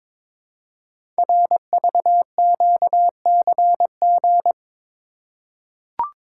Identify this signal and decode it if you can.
Morse code, heard as a keyed tone, about 22 wpm, 700 Hz. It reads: RVQCG
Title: Code Groups - 5 Letters - 22wpm